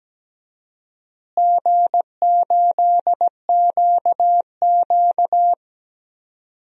Morse code, heard as a keyed tone, about 17 wpm, 700 Hz. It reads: G8QQ
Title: DXCC-100 Call Signs - 4 Characters - Repeated 2x - 17wpm